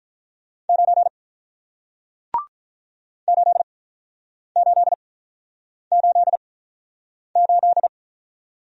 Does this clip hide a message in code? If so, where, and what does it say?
Yes: Morse code from 0.7 to 7.9 s, heard as a keyed tone, about 48 wpm, spaced as for 9 wpm, 700 Hz. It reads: 9 8888